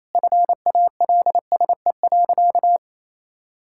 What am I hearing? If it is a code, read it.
Morse code, heard as a keyed tone, about 28 wpm, 700 Hz. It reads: FALSE.
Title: Sentences from Top 1600 Words 28wpm